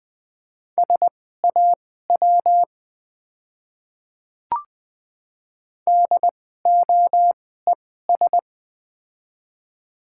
Morse code, heard as a keyed tone, about 20 wpm, spaced as for 14 wpm, 700 Hz. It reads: SAW DOES
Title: 2x Character Spacing - Top 300 Words - 20wpm